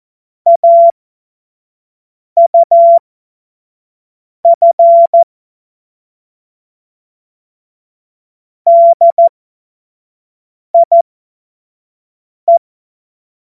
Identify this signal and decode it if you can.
Morse code, heard as a keyed tone, about 14 wpm, spaced as for 5 wpm, 680 Hz. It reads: AUF DIE
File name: Morsekurs / Witze / Witz 05 März 2024 25 F70